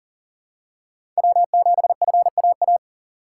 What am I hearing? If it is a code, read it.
Morse code, heard as a keyed tone, about 40 wpm, 700 Hz. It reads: W7FAA
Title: USA Call Signs - 5 Characters - 3x Character Spacing - 40wpm